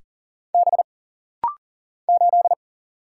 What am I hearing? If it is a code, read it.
Morse code, heard as a keyed tone, about 40 wpm, 700 Hz. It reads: B 8
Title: Single Letter-Number Alt - Rapid-Fire 40wpm